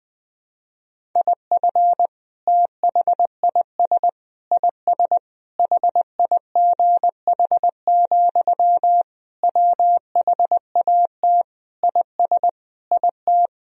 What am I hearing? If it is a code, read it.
Morse code, heard as a keyed tone, about 20 wpm, 700 Hz. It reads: IF THIS IS HIGH, WHAT IS IT
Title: Sentences from Top 1000 Words - Review Part 1 - 20wpm